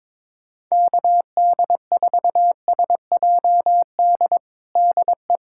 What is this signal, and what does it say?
Morse code, heard as a keyed tone, about 22 wpm, 700 Hz. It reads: KD4SJD DE
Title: USA Callsign DE Callsign 22wpm